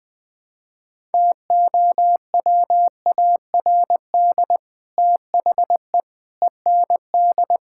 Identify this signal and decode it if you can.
Morse code, heard as a keyed tone, about 21 wpm, 700 Hz. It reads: TOWARD THE END
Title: Build A Phrase - 5-Word Phrases - 20wpm